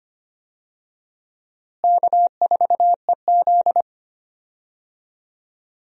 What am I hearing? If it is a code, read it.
Morse code, heard as a keyed tone, about 25 wpm, 700 Hz. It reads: K4EZ
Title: USA Call Signs - 4 Characters - Repeated 3x - 25wpm